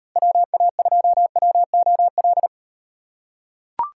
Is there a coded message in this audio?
Yes: Morse code from 0.2 to 2.5 s, heard as a keyed tone, about 38 wpm, 700 Hz. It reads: WA2WOL